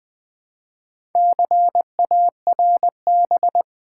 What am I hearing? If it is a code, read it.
Morse code, heard as a keyed tone, about 20 wpm, 700 Hz. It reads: CARB
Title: Build A Word - 2 Letters At A Time - 6-Letter Words - 20wpm